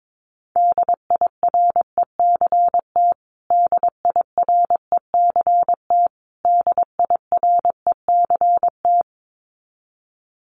Morse code, heard as a keyed tone, about 22 wpm, 700 Hz. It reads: DIRECT DIRECT DIRECT
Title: Top 500 Words Repeated 22wpm